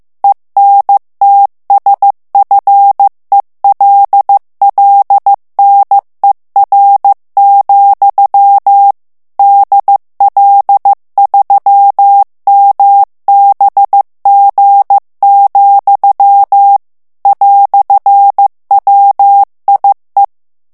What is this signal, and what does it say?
Morse code, heard as a keyed tone, about 15 wpm, 800 Hz. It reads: ENTSFELLNER, DL3MBG, "WIE
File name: morse mp3 / ALLEMAND texte clair 1 vitesse 14 mots minute